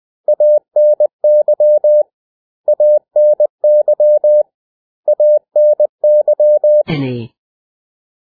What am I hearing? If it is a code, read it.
Morse code, heard as a keyed tone, about 20 wpm, 600 Hz. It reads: ANY ANY ANY
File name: cw / 100 Common English Words